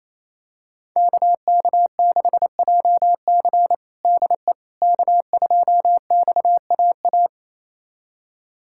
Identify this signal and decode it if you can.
Morse code, heard as a keyed tone, about 28 wpm, 700 Hz. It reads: KK6JC DE K2XAA
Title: USA Callsign DE Callsign - 5 Characters - 28wpm